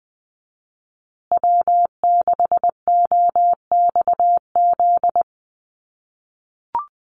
Morse code, heard as a keyed tone, about 20 wpm, 700 Hz. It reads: W6OXZ
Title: USA Call Signs Repeated 2x 20wpm